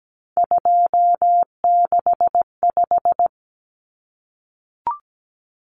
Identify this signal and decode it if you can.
Morse code, heard as a keyed tone, about 17 wpm, 700 Hz. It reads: 265
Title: Numbers - 3-Digit 17wpm